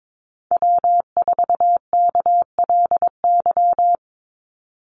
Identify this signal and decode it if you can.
Morse code, heard as a keyed tone, about 22 wpm, 700 Hz. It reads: W4KLY
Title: Call Signs - 22wpm